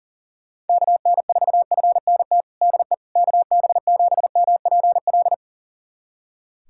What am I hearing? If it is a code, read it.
Morse code, heard as a keyed tone, about 40 wpm, 700 Hz. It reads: KN4FNT DE KB7MPL
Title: USA Callsign DE Callsign - 6 Characters - 40wpm